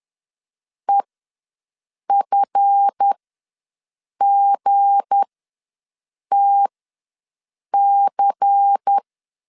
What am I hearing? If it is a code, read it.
Morse code, heard as a keyed tone, about 10 wpm, spaced as for 6 wpm, 800 Hz. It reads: EFGTC
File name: LEZIONI / 7 GF9Z / 7 GF ws 10wpm 3918